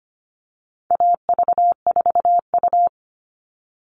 Morse code, heard as a keyed tone, about 25 wpm, 700 Hz. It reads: AV4U
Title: All Cut Numbers - 4-Digit 25wpm